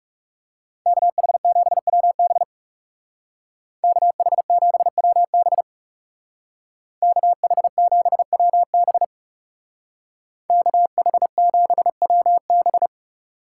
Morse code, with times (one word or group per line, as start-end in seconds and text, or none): 0.9-2.4 KH7WB
3.8-5.6 KH7WB
7.0-9.0 KH7WB
10.5-12.9 KH7WB